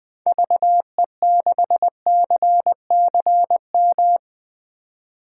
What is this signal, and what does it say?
Morse code, heard as a keyed tone, about 20 wpm, 700 Hz. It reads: VE6CCM
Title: 2023 SOTA Chaser Call Signs - 20wpm